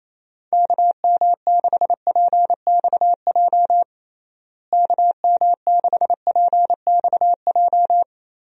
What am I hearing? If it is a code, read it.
Morse code, heard as a keyed tone, about 28 wpm, 700 Hz. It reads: KM6PXJ KM6PXJ
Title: USA Call Signs - 6 Characters - Repeated 3x - 28wpm